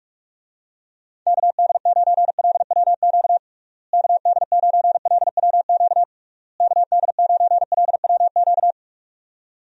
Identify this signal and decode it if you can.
Morse code, heard as a keyed tone, about 46 wpm, 700 Hz. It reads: KD9LWQ KD9LWQ KD9LWQ
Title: USA Call Signs - 6 Characters - Repeated 3x - 45wpm